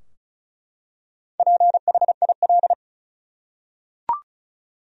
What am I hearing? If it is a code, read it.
Morse code, heard as a keyed tone, about 36 wpm, 700 Hz. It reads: PHIL
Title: USA Common Names - Max 4 Letters - 35wpm